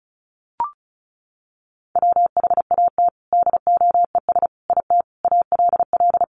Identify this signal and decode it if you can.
Morse code, heard as a keyed tone, about 34 wpm, 700 Hz. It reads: WHAT DOES IT ALL
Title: Sentences from Top 200 Words 35wpm